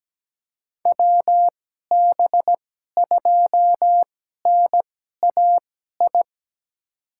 Morse code, heard as a keyed tone, about 17 wpm, spaced as for 12 wpm, 700 Hz. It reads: WB2NAI
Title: USA Call Signs - 6 Characters - 2x Character Spacing - 17wpm